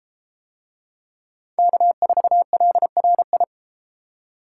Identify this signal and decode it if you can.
Morse code, heard as a keyed tone, about 32 wpm, 700 Hz. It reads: K4LRI